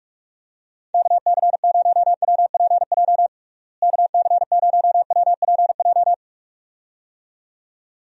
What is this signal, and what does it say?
Morse code, heard as a keyed tone, about 42 wpm, 700 Hz. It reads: KC0WPJ KC0WPJ
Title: USA Call Signs - 6 Characters - Repeated 2x - 45wpm